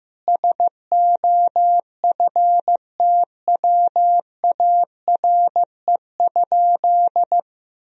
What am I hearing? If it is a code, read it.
Morse code, heard as a keyed tone, about 14 wpm, 700 Hz. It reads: SOFTWARE?
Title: Sentences from Top 1000 Words - Review Part 3 - 15wpm